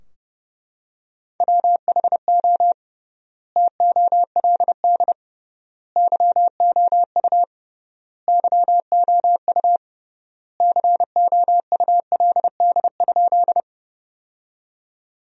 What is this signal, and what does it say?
Morse code, heard as a keyed tone, about 32 wpm, 700 Hz. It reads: WHO TOLD YOU YOU COULD?
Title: 3x Word-Spacing - Sentences from Top 400 Words - 30wpm